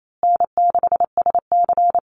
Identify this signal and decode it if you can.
Morse code, heard as a keyed tone, about 28 wpm, 700 Hz. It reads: N6SC
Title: USA Call Signs Repeated 3x 28wpm